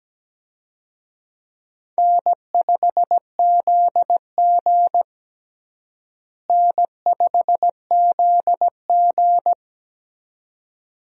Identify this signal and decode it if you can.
Morse code, heard as a keyed tone, about 17 wpm, 700 Hz. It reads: N5ZG N5ZG
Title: USA Call Signs - 4 Characters - Repeated 2x - 17wpm